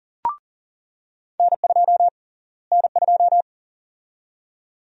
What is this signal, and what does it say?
Morse code, heard as a keyed tone, about 40 wpm, 700 Hz. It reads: N2 N2
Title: USA Call Sign Prefixes - Repeated 2x - 40wpm